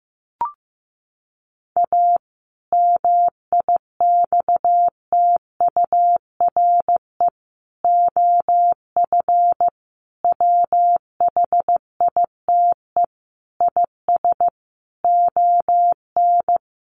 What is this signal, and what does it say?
Morse code, heard as a keyed tone, about 15 wpm, 700 Hz. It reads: A MIXTURE OF WHITE IS ON
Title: Sentences from Top 1500 Words 15wpm